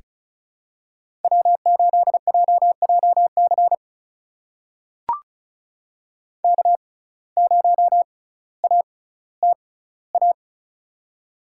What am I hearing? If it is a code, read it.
Morse code, heard as a keyed tone, about 36 wpm, spaced as for 18 wpm, 700 Hz. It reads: W8JJC K0ATA
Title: USA Call Signs - 5 Characters - 6x Character Spacing - 35wpm